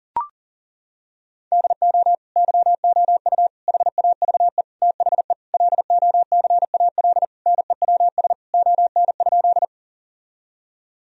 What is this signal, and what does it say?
Morse code, heard as a keyed tone, about 40 wpm, 700 Hz. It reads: DO YOU HAVE THE LOCAL NEWS ON?
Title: Sentences from Top 1000 Words - Review Part 3 - 40wpm